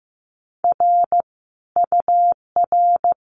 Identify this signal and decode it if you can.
Morse code, heard as a keyed tone, about 15 wpm, 700 Hz. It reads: R UR
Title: Common QSO Phrases Encore 15wpm